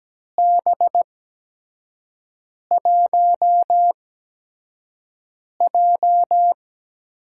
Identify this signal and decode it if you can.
Morse code, heard as a keyed tone, about 17 wpm, spaced as for 5 wpm, 700 Hz. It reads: B1J
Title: USA Call Signs - 4 Characters - 8x Character Spacing - 17wpm